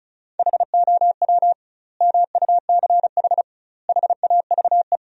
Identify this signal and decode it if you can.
Morse code, heard as a keyed tone, about 34 wpm, 700 Hz. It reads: HOW MUCH HAVE
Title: 2x Word-Spacing - Sentences from Top 300 Words - 35wpm